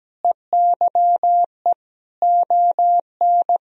Morse code, heard as a keyed tone, about 17 wpm, 700 Hz. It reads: EYE ON
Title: Sentences from Top 1000 Words - Review Part 1 - 17wpm